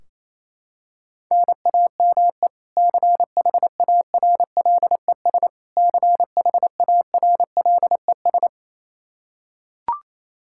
Morse code, heard as a keyed tone, about 27 wpm, 700 Hz. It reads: NAME CHARLES CHARLES
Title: Practice QSO - Exchange 1 - Beginning - 28wpm